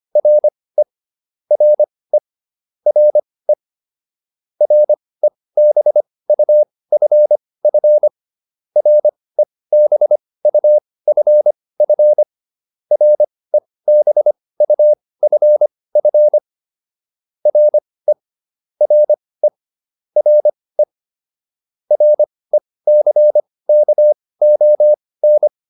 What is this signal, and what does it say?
Morse code, heard as a keyed tone, about 25 wpm, spaced as for 18 wpm, 600 Hz. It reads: RE RE RE REBUFF REBUFF REBUFF RE RE RE RECKON